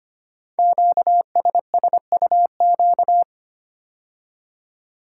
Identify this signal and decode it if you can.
Morse code, heard as a keyed tone, about 25 wpm, 700 Hz. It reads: QSSUQ
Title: Code Groups - 5 Letters - 25wpm